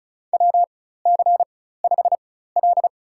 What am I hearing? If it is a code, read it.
Morse code, heard as a keyed tone, about 35 wpm, spaced as for 16 wpm, 700 Hz. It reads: WC5L
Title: USA Call Signs - 4 Characters - 4x Character Spacing - 35wpm